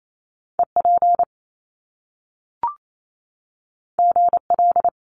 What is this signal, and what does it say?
Morse code, heard as a keyed tone, about 28 wpm, 700 Hz. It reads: EP GL
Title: Top Two Letter Combinations in Words 28wpm